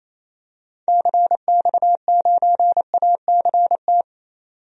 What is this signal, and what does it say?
Morse code, heard as a keyed tone, about 28 wpm, 700 Hz. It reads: CX9ACT
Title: DXCC-100 Call Signs - 6 Characters - 28wpm